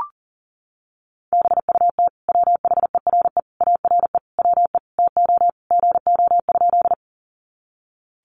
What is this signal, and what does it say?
Morse code, heard as a keyed tone, about 40 wpm, 700 Hz. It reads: BUT WHERE ARE WE TO GO?
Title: Sentences from Top 200 Words 40wpm